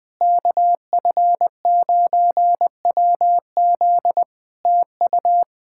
Morse code, heard as a keyed tone, about 20 wpm, 700 Hz. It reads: KF9WZ TU
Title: K1USN SST QSO Practice 20wpm